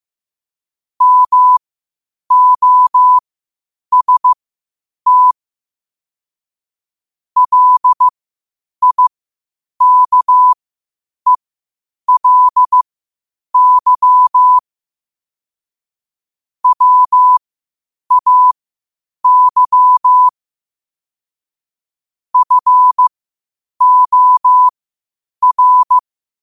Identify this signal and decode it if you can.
Morse code, heard as a keyed tone, about 15 wpm, spaced as for 8 wpm, 1000 Hz. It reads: MOST LIKELY WAY FOR
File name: qotdfiles / 8wpm